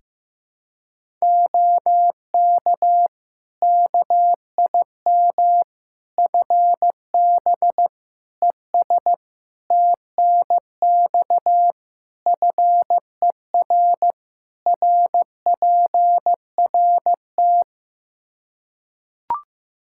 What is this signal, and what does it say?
Morse code, heard as a keyed tone, about 15 wpm, 700 Hz. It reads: OK KIM FB ES TNX FER RPRT
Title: Practice QSO - Exchange 2 - Info - 15wpm